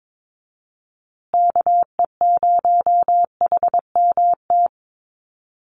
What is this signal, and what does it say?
Morse code, heard as a keyed tone, about 22 wpm, 700 Hz. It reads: KE0HMT
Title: USA Call Signs Repeated 2x 22wpm